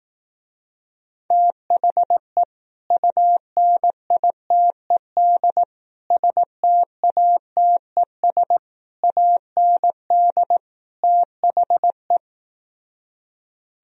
Morse code, heard as a keyed tone, about 18 wpm, 700 Hz. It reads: THE UNITED STATES AND THE
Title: Speed-Racing - Sets of 5 Words 15wpm